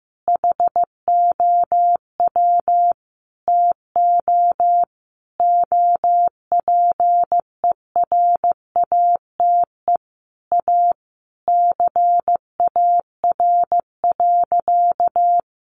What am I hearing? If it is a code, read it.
Morse code, heard as a keyed tone, about 15 wpm, 700 Hz. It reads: HOW TO OPERATE A CAR.
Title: Sentences from Top 1500 Words 15wpm